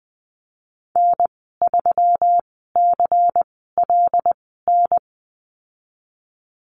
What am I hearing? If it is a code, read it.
Morse code, heard as a keyed tone, about 20 wpm, spaced as for 15 wpm, 700 Hz. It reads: N3CLN
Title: 122 Lesson 20g - Number 3 - Call Signs 20wpm